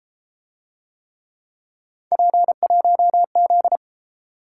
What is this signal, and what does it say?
Morse code, heard as a keyed tone, about 30 wpm, 700 Hz. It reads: P1Z